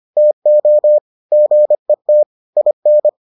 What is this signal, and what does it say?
Morse code, heard as a keyed tone, about 25 wpm, 600 Hz. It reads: TO GET IN